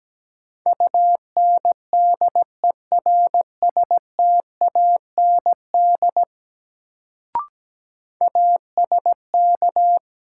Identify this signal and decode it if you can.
Morse code, heard as a keyed tone, about 17 wpm, 700 Hz. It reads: UNDERSTAND ASK
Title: Sets of 4 Words - New Word Drill - 17wpm